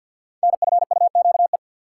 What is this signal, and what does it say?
Morse code, heard as a keyed tone, about 52 wpm, 700 Hz. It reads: NFUQE
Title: Code Groups - 5 Letter:Numbers - 50wpm